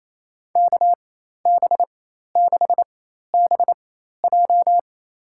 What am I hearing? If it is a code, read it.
Morse code, heard as a keyed tone, about 28 wpm, spaced as for 13 wpm, 700 Hz. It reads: KB6BJ
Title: USA Call Signs - 4x Character Spacing - 28wpm